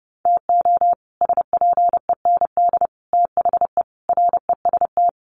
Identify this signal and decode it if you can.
Morse code, heard as a keyed tone, about 30 wpm, 700 Hz. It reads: TO SPEND THE REST
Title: Sets of 5 Words Encore2 30wpm